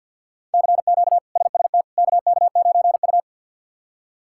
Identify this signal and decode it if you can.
Morse code, heard as a keyed tone, about 53 wpm, 700 Hz. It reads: CQ SST KK9U
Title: K1USN SST QSO Practice - 50wpm